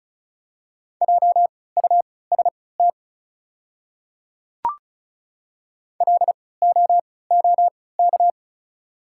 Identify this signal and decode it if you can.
Morse code, heard as a keyed tone, about 36 wpm, spaced as for 20 wpm, 700 Hz. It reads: JUST LOOK
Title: 3x Character Spacing - Top 200 Words - 35wpm